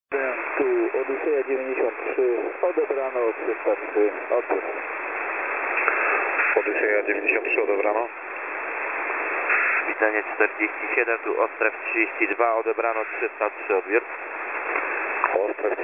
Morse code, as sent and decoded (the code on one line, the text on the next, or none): none